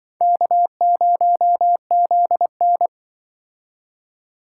Morse code, heard as a keyed tone, about 24 wpm, 700 Hz. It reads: K0ZN
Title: Speed Racing - USA 1x2 Call Signs 20wpm